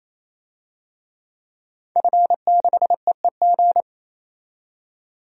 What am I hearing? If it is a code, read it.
Morse code, heard as a keyed tone, about 28 wpm, 700 Hz. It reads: F6EEG
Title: DXCC-100 Call Signs - 5 Characters - 28wpm